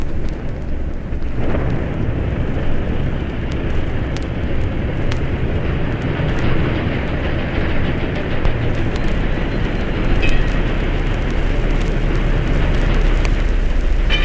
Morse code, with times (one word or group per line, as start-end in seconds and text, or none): none